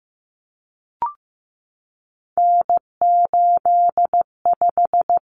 Morse code, heard as a keyed tone, about 15 wpm, 700 Hz. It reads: N85
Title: Common Cut Numbers - 3-Digit 15wpm